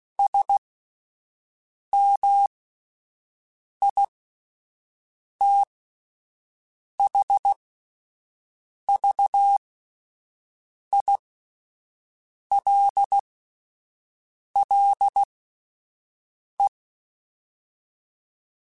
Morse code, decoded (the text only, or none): SMITHVILLE